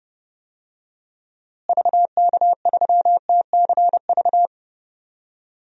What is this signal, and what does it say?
Morse code, heard as a keyed tone, about 29 wpm, 700 Hz. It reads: VK3TCV
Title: DXCC-100 Call Signs - 6 Characters - Repeated 2x - 30wpm